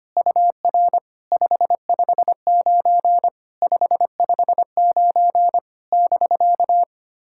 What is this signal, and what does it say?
Morse code, heard as a keyed tone, about 25 wpm, 700 Hz. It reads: UR 559 559 <BK>
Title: SOTA QSO Practice 25wpm